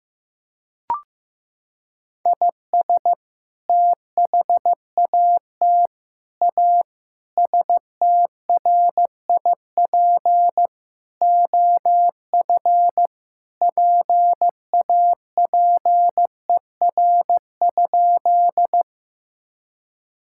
Morse code, heard as a keyed tone, about 16 wpm, 700 Hz. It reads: IS THAT A STRIP OF PAPER?
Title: Sentences from Top 1600 Words 15wpm